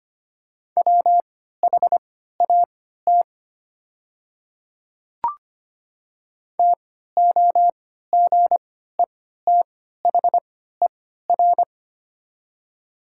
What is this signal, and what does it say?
Morse code, heard as a keyed tone, about 25 wpm, spaced as for 14 wpm, 700 Hz. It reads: WHAT TOGETHER
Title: 3x Character Spacing - Top 300 Words - 25wpm